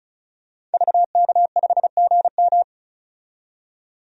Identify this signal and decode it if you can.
Morse code, heard as a keyed tone, about 35 wpm, 700 Hz. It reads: VK5GM
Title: DXCC-100 Call Signs - 5 Characters - 35wpm